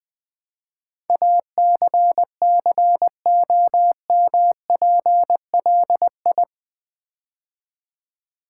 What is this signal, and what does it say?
Morse code, heard as a keyed tone, about 20 wpm, 700 Hz. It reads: ACCOMPLI
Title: Build A Word - 4 Letters At A Time - 10-Letter Words - 20wpm